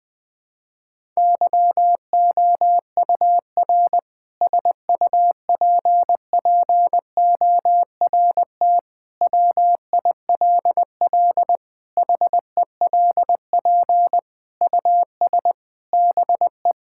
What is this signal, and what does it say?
Morse code, heard as a keyed tone, about 20 wpm, 700 Hz. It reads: YOUR SUPPORT WILL HELP US BE
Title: Sentences with 7-Word Phrases - 20wpm